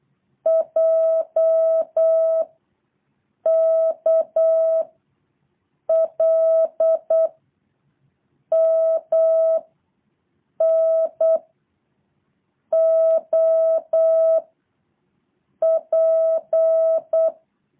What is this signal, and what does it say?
Morse code, heard as a keyed tone, about 8 wpm, 660 Hz. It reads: J K L M N O P
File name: jh8cbh / 2023cw